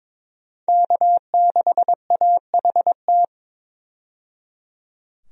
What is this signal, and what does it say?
Morse code, heard as a keyed tone, about 22 wpm, 700 Hz. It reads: K6AHT
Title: USA Call Signs - 5 Characters - 22wpm